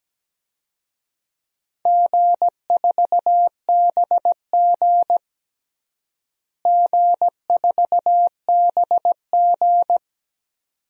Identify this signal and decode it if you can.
Morse code, heard as a keyed tone, about 17 wpm, 700 Hz. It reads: G4BG G4BG
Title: DXCC-100 Call Signs - 4 Characters - Repeated 2x - 17wpm